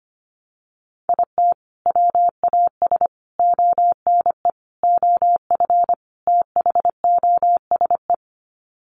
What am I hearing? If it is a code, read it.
Morse code, heard as a keyed tone, about 25 wpm, 700 Hz. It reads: IT WAS ONE OF THOSE
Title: Sets of 5 Words Encore 25wpm